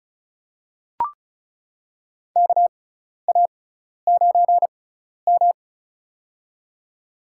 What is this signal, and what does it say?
Morse code, heard as a keyed tone, about 36 wpm, spaced as for 12 wpm, 700 Hz. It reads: KA9M
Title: USA Call Signs - 4 Characters - 6x Character Spacing - 35wpm